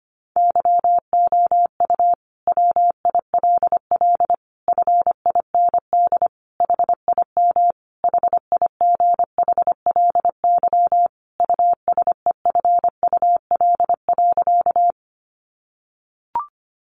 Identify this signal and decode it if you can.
Morse code, heard as a keyed tone, about 25 wpm, 700 Hz. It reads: YOU WILL FIND HIM HIGHLY USEFUL.
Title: Sentences from Top 1100 Words 25wpm